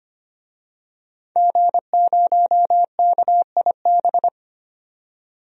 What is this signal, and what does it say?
Morse code, heard as a keyed tone, about 25 wpm, 700 Hz. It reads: G0KIB
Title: DXCC-100 Call Signs - 5 Characters - 25wpm